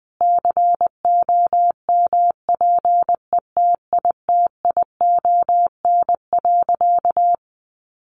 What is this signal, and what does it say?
Morse code, heard as a keyed tone, about 20 wpm, 700 Hz. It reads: COMPETITION.
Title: Sentences from Top 1000 Words 20wpm